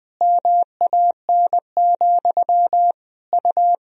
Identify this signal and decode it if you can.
Morse code, heard as a keyed tone, about 20 wpm, 700 Hz. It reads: MAN, U
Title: Sentences from Top 400 Words and Abbreviations 20wpm